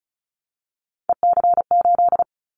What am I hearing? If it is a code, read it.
Morse code, heard as a keyed tone, about 35 wpm, 700 Hz. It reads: EC8
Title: Callsign Prefixes for Top 50 DXCC - 35wpm